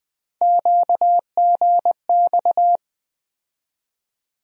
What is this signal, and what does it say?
Morse code, heard as a keyed tone, about 20 wpm, 700 Hz. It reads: QGX
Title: USA Call Sign Postfixes - 3 Letters 20wpm